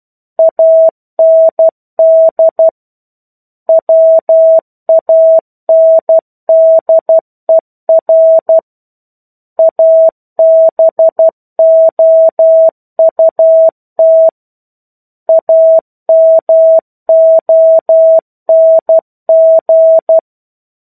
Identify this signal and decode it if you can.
Morse code, heard as a keyed tone, about 12 wpm, 650 Hz. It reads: AND WANDER ABOUT AMONG
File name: 01-Chapter-01 12 WPM